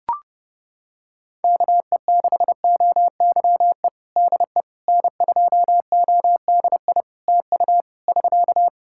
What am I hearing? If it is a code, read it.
Morse code, heard as a keyed tone, about 30 wpm, 700 Hz. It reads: KE6OYE DE N2ODI TU <SK>
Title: K1USN SST QSO Practice 30wpm